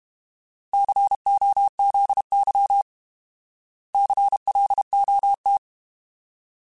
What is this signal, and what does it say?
Morse code, heard as a keyed tone, about 32 wpm, 800 Hz. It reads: COZY CLOT
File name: Alphabetical / Alphabetical mp3s / Cs @30